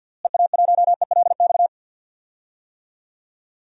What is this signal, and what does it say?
Morse code, heard as a keyed tone, about 50 wpm, 700 Hz. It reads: EA1ELX
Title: DXCC-100 Call Signs - 6 Characters - Repeated 2x - 50wpm